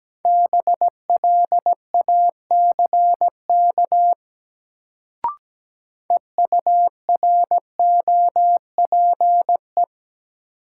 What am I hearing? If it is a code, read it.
Morse code, heard as a keyed tone, about 17 wpm, 700 Hz. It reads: BLACK EUROPE
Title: Sets of 5 Words Encore - New Word Drill - 17wpm